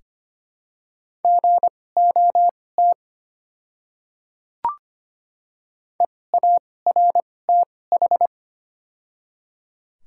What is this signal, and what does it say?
Morse code, heard as a keyed tone, about 25 wpm, spaced as for 18 wpm, 700 Hz. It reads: GOT EARTH
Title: 2x Character Spacing - Top 300 Words - 25wpm